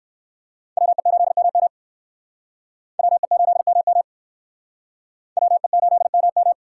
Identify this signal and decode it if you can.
Morse code, heard as a keyed tone, about 57 wpm, 700 Hz. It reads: WE8MM WE8MM WE8MM